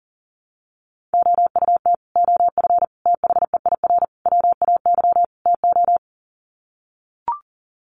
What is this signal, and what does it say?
Morse code, heard as a keyed tone, about 40 wpm, 700 Hz. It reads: OUT OF THEIR WAY TO
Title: Sets of 5 Words Encore2 40wpm